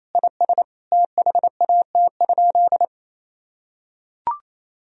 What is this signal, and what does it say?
Morse code, heard as a keyed tone, about 28 wpm, 700 Hz. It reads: IS THAT?
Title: Sentences from Top 1000 Words - Review Part 4 - 28wpm